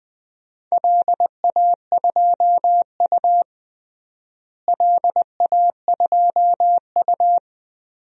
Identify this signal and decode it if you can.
Morse code, heard as a keyed tone, about 20 wpm, 700 Hz. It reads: LA2U LA2U
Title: DXCC-100 Call Signs - 4 Characters - Repeated 2x - 20wpm